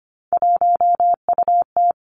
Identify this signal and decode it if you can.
Morse code, heard as a keyed tone, about 25 wpm, 700 Hz. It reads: 1UT
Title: Three Challenging Letter-Number Combinations 25wpm